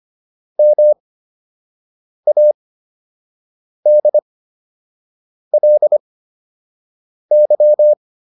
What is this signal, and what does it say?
Morse code, heard as a keyed tone, about 25 wpm, spaced as for 6 wpm, 600 Hz. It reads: MADLY